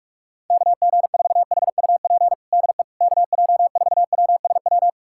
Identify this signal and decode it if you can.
Morse code, heard as a keyed tone, about 46 wpm, 700 Hz. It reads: KG4HUP DE KJ4WSW
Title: USA Callsign DE Callsign - 6 Characters - 45wpm